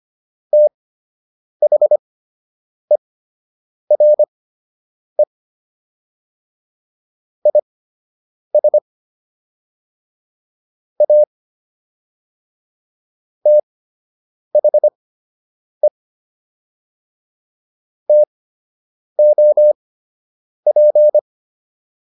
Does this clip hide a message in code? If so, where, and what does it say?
Yes: Morse code from 0.5 to 21.2 s, heard as a keyed tone, about 25 wpm, spaced as for 8 wpm, 600 Hz. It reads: THERE IS A THE TOP